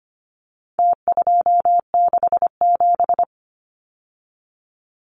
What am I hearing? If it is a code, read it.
Morse code, heard as a keyed tone, about 25 wpm, 700 Hz. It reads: T267
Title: Numbers - Contest Style - 4-Digit 25wpm